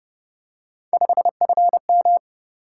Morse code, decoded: 5FM

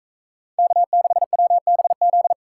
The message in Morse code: K6WBZ

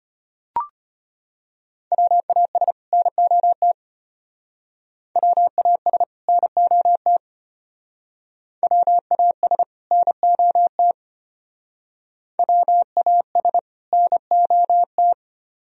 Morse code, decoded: WAS NOT WAS NOT WAS NOT WAS NOT

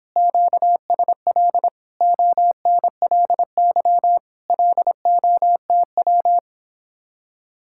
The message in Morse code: QSL ONLY LOTW

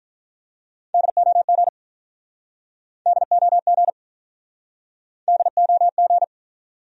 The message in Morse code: DOG DOG DOG

